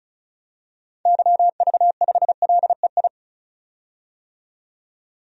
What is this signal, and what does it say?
Morse code, heard as a keyed tone, about 35 wpm, 700 Hz. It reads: YV5LEI
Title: DXCC-100 Call Signs - 6 Characters - 35wpm